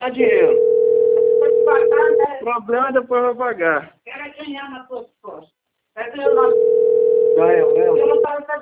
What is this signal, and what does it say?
Ringback tone (North America), 440 + 480 Hz, with speech.
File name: out- - - - - .15003